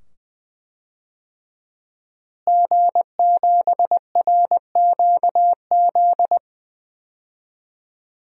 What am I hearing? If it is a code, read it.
Morse code, heard as a keyed tone, about 20 wpm, 700 Hz. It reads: G7RQZ
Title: DXCC-100 Call Signs - 5 Characters - Repeated 2x - 20wpm